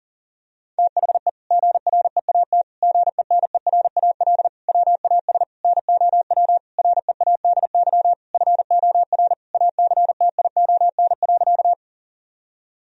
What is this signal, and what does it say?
Morse code, heard as a keyed tone, about 40 wpm, 700 Hz. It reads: THE GREAT GENERAL WAS NOW READY FOR ACTION.
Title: Sentences from Top 1000 Words - Review Part 4 - 40wpm